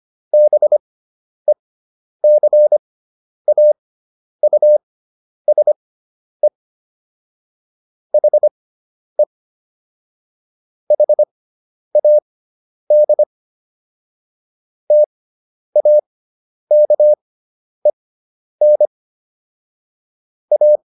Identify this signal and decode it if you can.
Morse code, heard as a keyed tone, about 25 wpm, spaced as for 10 wpm, 600 Hz. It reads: BECAUSE HE HAD TAKEN A